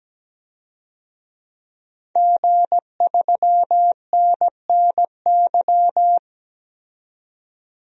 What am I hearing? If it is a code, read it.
Morse code, heard as a keyed tone, about 17 wpm, 700 Hz. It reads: G3NNY
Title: DXCC-100 Call Signs - 5 Characters - 17wpm